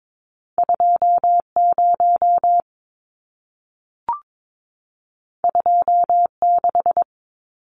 Morse code, heard as a keyed tone, about 22 wpm, 700 Hz. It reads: 20 26
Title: Numbers - 2-Digit 22wpm